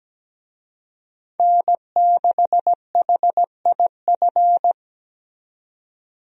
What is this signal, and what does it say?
Morse code, heard as a keyed tone, about 17 wpm, 700 Hz. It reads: N6HIF
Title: USA Call Signs - 5 Characters - 17wpm